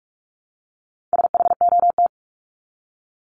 Morse code, heard as a keyed tone, about 44 wpm, 700 Hz. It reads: SHOT
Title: Top 1001-1250 Words 45wpm